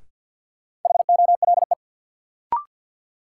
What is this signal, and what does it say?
Morse code, heard as a keyed tone, about 52 wpm, 700 Hz. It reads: HOLE
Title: Sentences from Top 1600 Words - New Word Drill - 50wpm